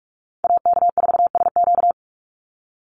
Morse code, heard as a keyed tone, about 44 wpm, 700 Hz. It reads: AK4SQ
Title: Call Signs Repeated 2x - 45wpm